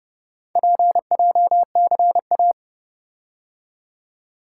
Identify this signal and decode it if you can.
Morse code, heard as a keyed tone, about 29 wpm, 700 Hz. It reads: PJCA